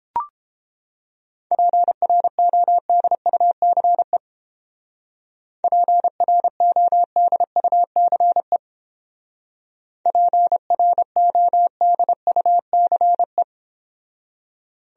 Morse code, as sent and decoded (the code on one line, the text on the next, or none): .--. .-. --- -.. ..- -.-. / .--. .-. --- -.. ..- -.-. / .--. .-. --- -.. ..- -.-. .
PRODUCE PRODUCE PRODUCE